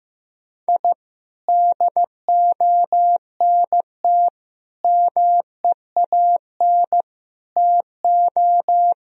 Morse code, heard as a keyed tone, about 16 wpm, 700 Hz. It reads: I DONT MEAN TO
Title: Sentences from Top 1000 Words - Review Part 1 - 15wpm